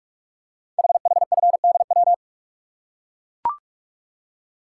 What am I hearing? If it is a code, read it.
Morse code, heard as a keyed tone, about 44 wpm, 690 Hz. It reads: HHFDW
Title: Code Groups - 5 Letters - 45wpm